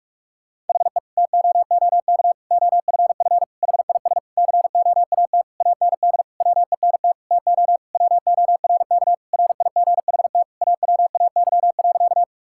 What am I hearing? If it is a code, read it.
Morse code, heard as a keyed tone, about 46 wpm, 700 Hz. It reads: HE TOOK OFF HIS COAT AND WENT TO WORK RIGHT AWAY.
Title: Sentences from Top 1600 Words 45wpm